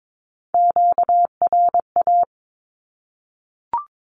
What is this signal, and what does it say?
Morse code, heard as a keyed tone, about 22 wpm, 700 Hz. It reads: QRA
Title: Q-Codes 22wpm